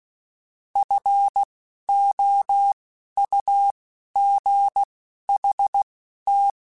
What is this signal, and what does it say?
Morse code, heard as a keyed tone, about 16 wpm, spaced as for 11 wpm, 800 Hz. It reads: FOUGHT
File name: F Runs / F Runs mp3s / F Runs @10